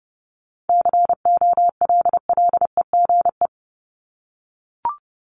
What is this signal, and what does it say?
Morse code, heard as a keyed tone, about 30 wpm, 700 Hz. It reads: COLLEGE
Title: Top 1001-1250 Words 30wpm